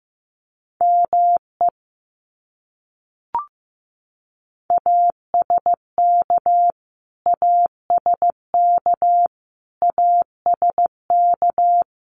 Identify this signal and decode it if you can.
Morse code, heard as a keyed tone, about 15 wpm, 700 Hz. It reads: ME ASK ASK ASK
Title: Top 200 Words Repeated 15wpm